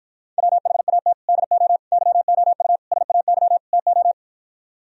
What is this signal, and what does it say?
Morse code, heard as a keyed tone, about 52 wpm, 700 Hz. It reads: WHAT DO YOU SAY TO